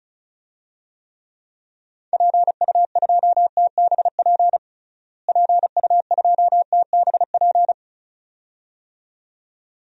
Morse code, heard as a keyed tone, about 34 wpm, 700 Hz. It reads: PU2TBP PU2TBP
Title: DXCC-100 Call Signs - 6 Characters - Repeated 2x - 35wpm